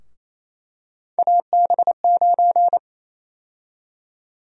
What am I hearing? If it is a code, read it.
Morse code, heard as a keyed tone, about 28 wpm, 700 Hz. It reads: AB9
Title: USA Call Sign Prefixes - 28wpm